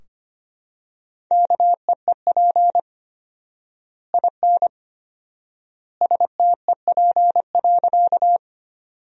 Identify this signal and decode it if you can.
Morse code, heard as a keyed tone, about 25 wpm, 700 Hz. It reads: KEEP IN STEP.
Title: 4x Word-Spacing - Sentences from Top 400 Words - 25wpm